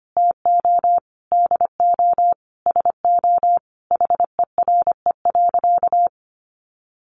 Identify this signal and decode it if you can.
Morse code, heard as a keyed tone, about 25 wpm, 700 Hz. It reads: TO DO SO HERE.
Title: Sentences from Top 600 Words 25wpm